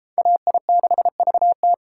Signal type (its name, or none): Morse code